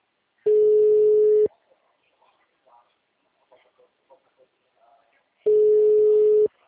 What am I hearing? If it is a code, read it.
Ringback tone (Europe, 425 Hz), with speech.
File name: out- -2020- - - .42310